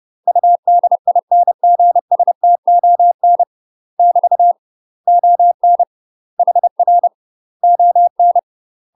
Morse code, heard as a keyed tone, about 30 wpm, 700 Hz. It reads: UDINGSTON <BT> ON HR ON
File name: UK MASTER QSOs TRAINING / 30WPM / UK MASTER QSOs 0013